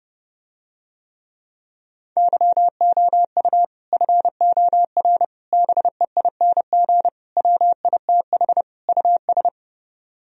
Morse code, heard as a keyed tone, about 30 wpm, 700 Hz. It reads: YOU FOR BEING WITH US